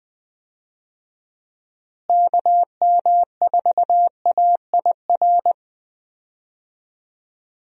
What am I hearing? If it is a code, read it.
Morse code, heard as a keyed tone, about 20 wpm, 700 Hz. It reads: KM4AIR
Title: USA Call Signs - 6 Characters - Repeated 2x - 20wpm